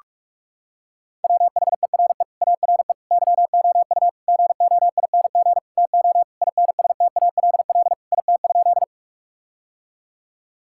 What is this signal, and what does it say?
Morse code, heard as a keyed tone, about 46 wpm, 700 Hz. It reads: WHERE ARE YOU GOING TO INSTALL IT?
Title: Sentences from Top 1600 Words 45wpm